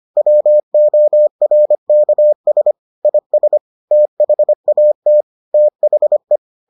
Morse code, heard as a keyed tone, about 25 wpm, 600 Hz. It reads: WORKS IS THAT THE